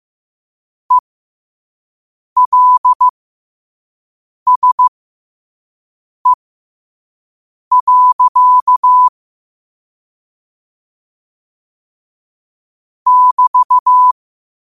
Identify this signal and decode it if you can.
Morse code, heard as a keyed tone, about 15 wpm, spaced as for 5 wpm, 1000 Hz. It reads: ELSE. <BT>
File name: qotdfiles / 5wpm